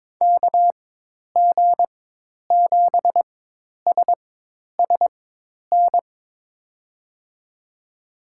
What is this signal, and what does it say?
Morse code, heard as a keyed tone, about 22 wpm, spaced as for 10 wpm, 700 Hz. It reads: KG7SSN
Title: USA Call Signs - 6 Characters - 4x Character Spacing - 22wpm